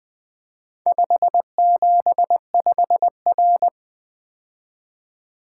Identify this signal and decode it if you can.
Morse code, heard as a keyed tone, about 20 wpm, 700 Hz. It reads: 575R
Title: Four Letter-Number Combinations - 20wpm